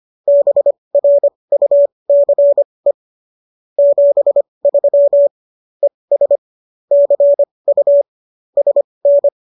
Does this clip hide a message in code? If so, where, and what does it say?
Yes: Morse code from 0.3 to 9.3 s, heard as a keyed tone, about 25 wpm, spaced as for 20 wpm, 570 Hz. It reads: BRUCE 73 ES CU SN